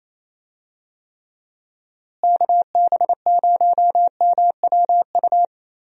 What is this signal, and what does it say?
Morse code, heard as a keyed tone, about 28 wpm, 700 Hz. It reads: KB0MWU
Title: USA Call Signs - 6 Characters - Repeated 3x - 28wpm